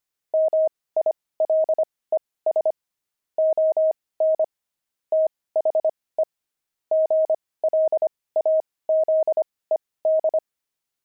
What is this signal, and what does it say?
Morse code, heard as a keyed tone, about 25 wpm, spaced as for 18 wpm, 630 Hz. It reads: MILES ON THE GLAZED
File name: pangram-fs-18-cs-25-pavement